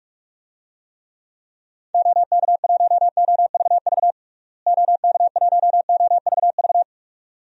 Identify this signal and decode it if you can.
Morse code, heard as a keyed tone, about 46 wpm, 700 Hz. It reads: OK1OVV OK1OVV
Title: DXCC-100 Call Signs - 6 Characters - Repeated 2x - 45wpm